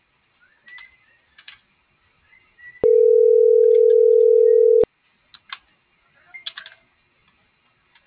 Ringback tone (North America), 440 + 480 Hz.